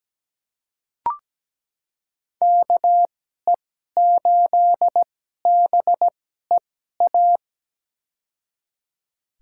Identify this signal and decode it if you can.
Morse code, heard as a keyed tone, about 17 wpm, spaced as for 12 wpm, 700 Hz. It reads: KE8BEA
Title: USA Call Signs - 2x Character Spacing - 17wpm